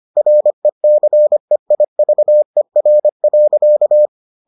Morse code, heard as a keyed tone, about 25 wpm, 600 Hz. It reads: RECEIVER.